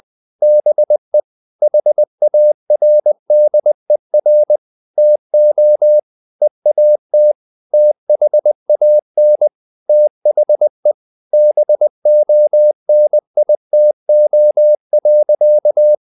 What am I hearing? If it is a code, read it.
Morse code, heard as a keyed tone, about 20 wpm, 600 Hz. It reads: BE HARDER TO EAT THAN THE BONITO.
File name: the old man and the sea part 2 20wpm